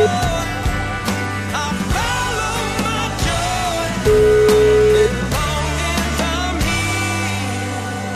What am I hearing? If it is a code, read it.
Ringback tone (Europe, 425 Hz), with noise.